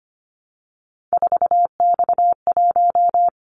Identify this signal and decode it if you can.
Morse code, heard as a keyed tone, about 25 wpm, 700 Hz. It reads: 4X1
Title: Callsign Prefixes for Top 50 DXCC - 25wpm